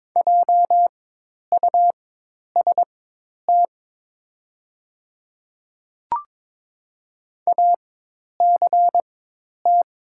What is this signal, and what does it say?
Morse code, heard as a keyed tone, about 23 wpm, spaced as for 10 wpm, 700 Hz. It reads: JUST ACT
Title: 4x Character Spacing - Top 200 Words - 22wpm